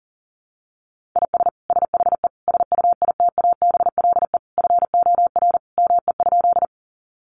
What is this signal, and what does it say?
Morse code, heard as a keyed tone, about 40 wpm, 700 Hz. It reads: IS SHE SUITABLE FOR ME?
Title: Sentences from Top 1400 Words 40wpm